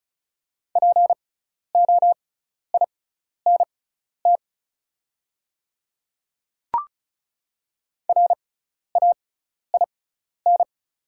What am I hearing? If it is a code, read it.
Morse code, heard as a keyed tone, about 36 wpm, spaced as for 12 wpm, 700 Hz. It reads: POINT RAIN
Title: 6x Character Spacing - Top 300 Words - 35wpm